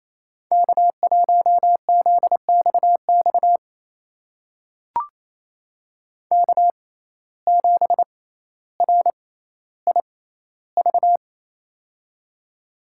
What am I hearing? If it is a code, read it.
Morse code, heard as a keyed tone, about 28 wpm, spaced as for 14 wpm, 700 Hz. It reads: K1ZXX K7RIV
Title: USA Call Signs - 5 Characters - 6x Character Spacing - 28wpm